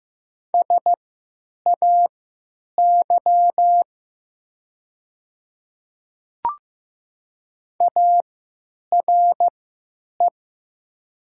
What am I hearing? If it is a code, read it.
Morse code, heard as a keyed tone, about 16 wpm, spaced as for 9 wpm, 700 Hz. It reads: SAY ARE
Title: 3x Character Spacing - Top 500 Words - 15wpm